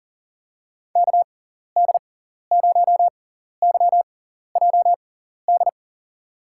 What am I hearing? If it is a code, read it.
Morse code, heard as a keyed tone, about 40 wpm, spaced as for 14 wpm, 700 Hz. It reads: KD0YJD
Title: USA Call Signs - 6x Character Spacing - 40wpm